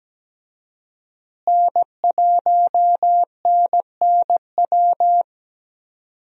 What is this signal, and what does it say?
Morse code, heard as a keyed tone, about 17 wpm, 700 Hz. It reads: N1NNW
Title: USA Call Signs - 5 Characters - Repeated 3x - 17wpm